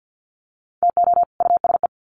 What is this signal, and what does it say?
Morse code, heard as a keyed tone, about 50 wpm, 700 Hz. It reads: TO USE
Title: Sets of 2 Words 50wpm